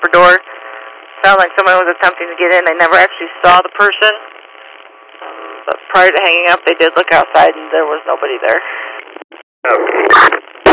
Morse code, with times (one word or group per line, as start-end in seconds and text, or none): none